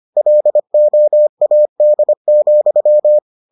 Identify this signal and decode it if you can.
Morse code, heard as a keyed tone, about 25 wpm, 600 Hz. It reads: LOAD,